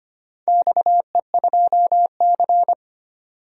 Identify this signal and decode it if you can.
Morse code, heard as a keyed tone, about 25 wpm, 700 Hz. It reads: XE2C
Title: DXCC-100 Call Signs - 4 Characters - 25wpm